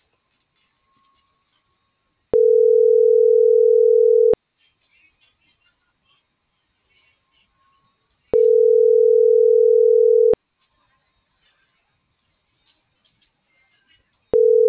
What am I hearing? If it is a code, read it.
Ringback tone (North America), 440 + 480 Hz.